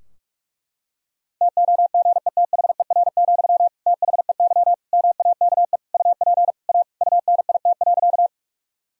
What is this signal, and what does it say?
Morse code, heard as a keyed tone, about 46 wpm, 700 Hz. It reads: TOGETHER, THEY MAKE UP A UNIT.
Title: Sentences from Top 1000 Words - Review Part 2 - 45wpm